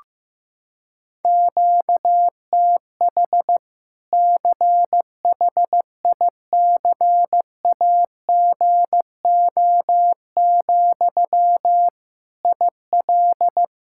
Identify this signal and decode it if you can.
Morse code, heard as a keyed tone, about 15 wpm, 700 Hz. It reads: QTH CHICAGO, IL
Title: Practice QSO - Exchange 1 - Beginning - 15wpm